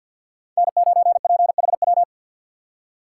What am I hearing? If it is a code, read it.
Morse code, heard as a keyed tone, about 64 wpm, 700 Hz. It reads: N9PHW